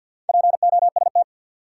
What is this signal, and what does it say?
Morse code, heard as a keyed tone, about 53 wpm, 700 Hz. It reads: POST
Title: Sentences from Top 1000 Words - New Word Drill - 50wpm